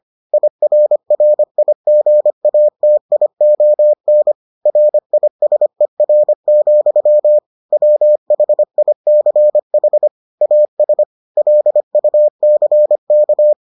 Morse code, heard as a keyed tone, about 25 wpm, 600 Hz. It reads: IRRIGATION RISER, WHICH AS LUCK